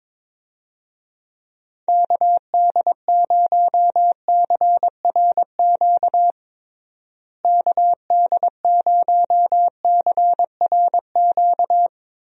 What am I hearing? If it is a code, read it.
Morse code, heard as a keyed tone, about 22 wpm, 700 Hz. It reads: KD0CRQ KD0CRQ
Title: USA Call Signs - 6 Characters - Repeated 3x - 22wpm